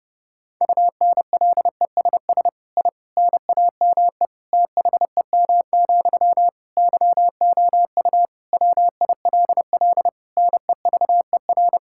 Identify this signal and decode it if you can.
Morse code, heard as a keyed tone, about 30 wpm, 700 Hz. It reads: UNLESS I NAME THEM, YOU WILL NEVER
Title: Sentences from Top 1000 Words - Review Part 6 - 30wpm